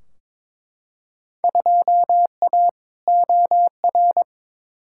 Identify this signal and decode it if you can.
Morse code, heard as a keyed tone, about 21 wpm, 700 Hz. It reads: 2A OR
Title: Field Day Exchanges - 22wpm